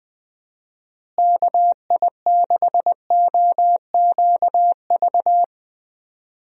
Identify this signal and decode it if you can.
Morse code, heard as a keyed tone, about 20 wpm, 700 Hz. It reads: KI6OQV
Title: USA Call Signs - 6 Characters - Repeated 2x - 20wpm